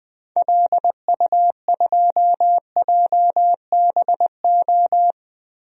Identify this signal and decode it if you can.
Morse code, heard as a keyed tone, about 20 wpm, 700 Hz. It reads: LU2JBO
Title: DXCC-100 Call Signs - 6 Characters - 20wpm